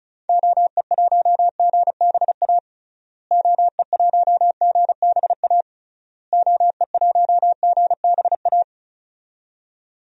Morse code, heard as a keyed tone, about 34 wpm, 700 Hz. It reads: OE1GBA OE1GBA OE1GBA
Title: DXCC-100 Call Signs - 6 Characters - Repeated 3x - 35wpm